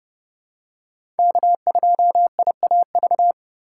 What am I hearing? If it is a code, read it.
Morse code, heard as a keyed tone, about 31 wpm, 700 Hz. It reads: K2IAV
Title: USA Call Signs - 5 Characters - 3x Character Spacing - 30wpm